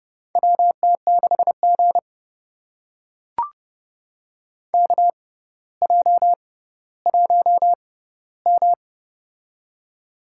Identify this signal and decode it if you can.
Morse code, heard as a keyed tone, about 30 wpm, spaced as for 15 wpm, 700 Hz. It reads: WT6G KJ1M
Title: USA Call Signs - 4 Characters - 6x Character Spacing - 30wpm